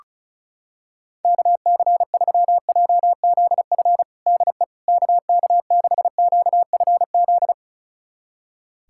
Morse code, heard as a keyed tone, about 36 wpm, 700 Hz. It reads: KC3JZF DE KK6QFZ
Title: USA Callsign DE Callsign 35wpm